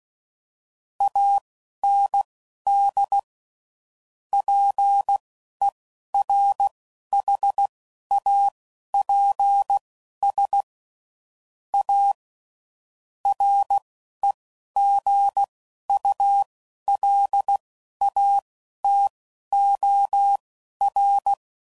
Morse code, heard as a keyed tone, about 16 wpm, spaced as for 12 wpm, 800 Hz. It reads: AND PERHAPS A REGULATOR
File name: CW Code Groups @10WPM / Technical Subjects / SwPwrSup